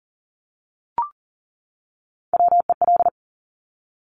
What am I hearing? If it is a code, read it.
Morse code, heard as a keyed tone, about 42 wpm, 700 Hz. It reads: WEL
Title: Top Three Letter Combinations in Words 40wpm